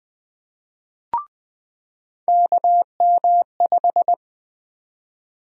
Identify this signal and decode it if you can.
Morse code, heard as a keyed tone, about 21 wpm, 700 Hz. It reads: KM5
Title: USA Call Sign Prefixes - 20wpm